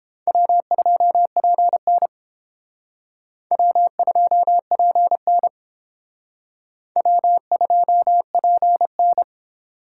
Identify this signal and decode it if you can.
Morse code, heard as a keyed tone, about 30 wpm, 700 Hz. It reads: W2PN W2PN W2PN